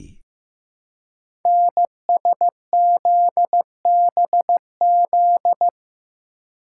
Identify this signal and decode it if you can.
Morse code, heard as a keyed tone, about 15 wpm, 700 Hz. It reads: NSZBZ
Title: Code Groups - 5 Letters - 15wpm